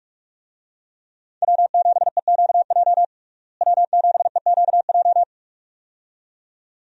Morse code, heard as a keyed tone, about 41 wpm, 700 Hz. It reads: W7EQJ W7EQJ
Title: USA Call Signs - 5 Characters - Repeated 2x - 45wpm